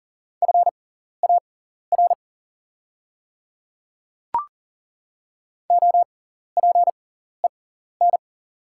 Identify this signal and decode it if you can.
Morse code, heard as a keyed tone, about 40 wpm, spaced as for 14 wpm, 690 Hz. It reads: FAR OPEN
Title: 6x Character Spacing - Top 300 Words - 40wpm